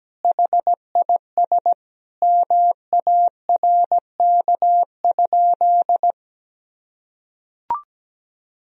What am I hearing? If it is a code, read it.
Morse code, heard as a keyed tone, about 17 wpm, 700 Hz. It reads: HIS MARK?
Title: Sentences from Top 300 Words and Abbreviations 17wpm